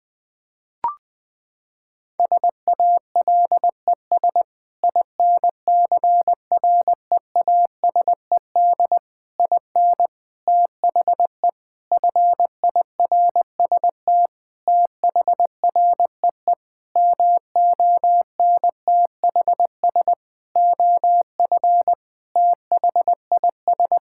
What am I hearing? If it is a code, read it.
Morse code, heard as a keyed tone, about 21 wpm, 700 Hz. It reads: SALES INCREASED IN THE FIRST THREE MONTHS OF THIS
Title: Sentences with 8-Word Phrases - 20wpm